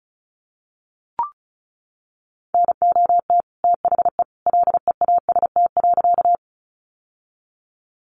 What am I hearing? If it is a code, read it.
Morse code, heard as a keyed tone, about 34 wpm, 700 Hz. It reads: NOT THE LEAST.
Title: Sentences from Top 700 Words 35wpm